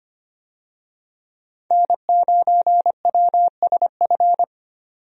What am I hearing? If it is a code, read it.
Morse code, heard as a keyed tone, about 25 wpm, 700 Hz. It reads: N9WSF
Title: USA Call Signs - 5 Characters - 25wpm